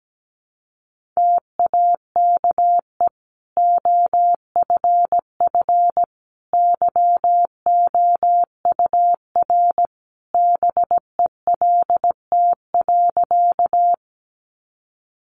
Morse code, heard as a keyed tone, about 17 wpm, 700 Hz. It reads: TAKE OFF YOUR BELT.
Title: Sentences from Top 1400 Words 17wpm